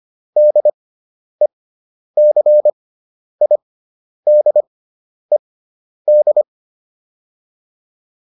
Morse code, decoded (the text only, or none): DECIDED